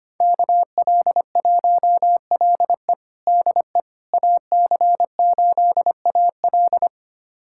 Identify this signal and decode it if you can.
Morse code, heard as a keyed tone, about 25 wpm, 700 Hz. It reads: KL1LE DE AC8AL
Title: USA Callsign DE Callsign - 5 Characters - 25wpm